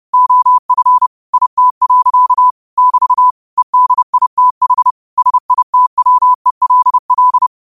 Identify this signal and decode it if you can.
Morse code, heard as a keyed tone, about 30 wpm, 1000 Hz. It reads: OF IT. <BT> EDITH SITWELL
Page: 30 WPM Morse code quotes